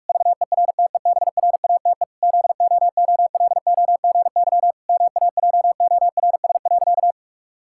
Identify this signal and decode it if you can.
Morse code, heard as a keyed tone, about 44 wpm, 700 Hz. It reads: VERTEBRATE ZOOLOGY MAJORS.
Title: Pangram Challenge v2 - 45wpm